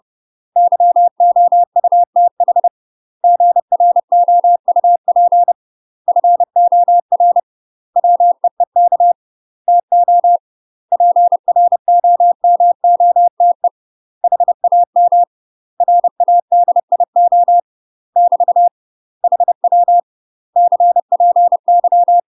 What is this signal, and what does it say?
Morse code, heard as a keyed tone, about 30 wpm, 700 Hz. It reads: YOUTH GROUP FOR WEEK TO PROMOTE HAM RADIO <BT> HW CPY
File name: UK MASTER QSOs TRAINING / 30WPM / UK MASTER QSOs 0013